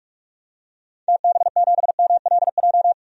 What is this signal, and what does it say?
Morse code, heard as a keyed tone, about 43 wpm, 700 Hz. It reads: TB7MLJ
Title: DXCC-100 Call Signs - 6 Characters - 45wpm